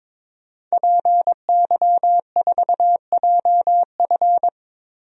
Morse code, heard as a keyed tone, about 22 wpm, 700 Hz. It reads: PY4JF
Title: DXCC-100 Call Signs - 5 Characters - 22wpm